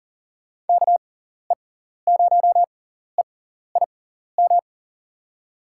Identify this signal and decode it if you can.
Morse code, heard as a keyed tone, about 39 wpm, spaced as for 14 wpm, 700 Hz. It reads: KE0EIM
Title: USA Call Signs - 6 Characters - 6x Character Spacing - 40wpm